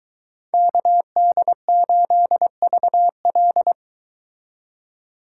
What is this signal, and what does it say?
Morse code, heard as a keyed tone, about 23 wpm, 700 Hz. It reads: KD8VL